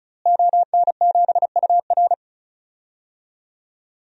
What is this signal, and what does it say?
Morse code, heard as a keyed tone, about 34 wpm, 700 Hz. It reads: ON7UR
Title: DXCC-100 Call Signs - 5 Characters - Repeated 3x - 35wpm